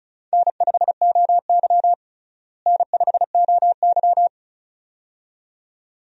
Morse code, heard as a keyed tone, about 34 wpm, 700 Hz. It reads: N5OY N5OY
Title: USA Call Signs - 4 Characters - Repeated 2x - 35wpm